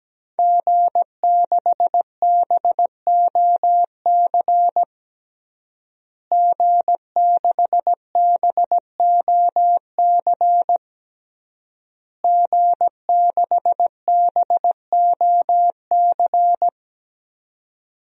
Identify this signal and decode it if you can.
Morse code, heard as a keyed tone, about 17 wpm, 700 Hz. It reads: G6BOC G6BOC G6BOC
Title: DXCC-100 Call Signs - 5 Characters - Repeated 3x - 17wpm